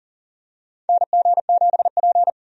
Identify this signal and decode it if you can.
Morse code, heard as a keyed tone, about 40 wpm, 700 Hz. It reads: NG7P
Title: USA Call Signs - 4 Characters - Repeated 2x - 40wpm